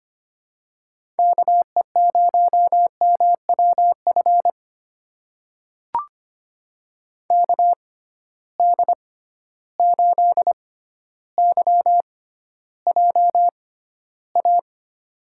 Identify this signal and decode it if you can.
Morse code, heard as a keyed tone, about 25 wpm, spaced as for 13 wpm, 700 Hz. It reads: KE0MWF KD8YJA
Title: USA Call Signs - 6 Characters - 6x Character Spacing - 25wpm